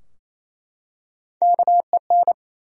Morse code, heard as a keyed tone, about 27 wpm, 700 Hz. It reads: KEN
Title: USA Common Names - Max 3 Letters - 28wpm